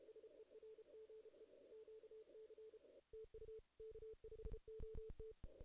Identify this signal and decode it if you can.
Morse code, heard as a keyed tone, about 33 wpm, 440 Hz. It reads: SERGE QC TU K5OT
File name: 2016 JAN CW NAQP / 2016 Jan naqp k5ot - 0-8